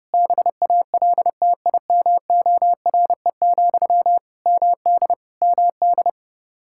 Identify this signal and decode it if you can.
Morse code, heard as a keyed tone, about 32 wpm, 700 Hz. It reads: BALTIMORE, MD MD
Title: Practice QSO - Exchange 1 - Beginning - 30wpm